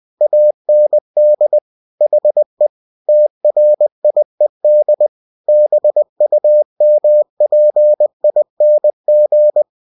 Morse code, heard as a keyed tone, about 20 wpm, 600 Hz. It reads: AND HE TRIED BUMPING
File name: cw-books / winnie-the-pooh / winnie-20wpm-page-0042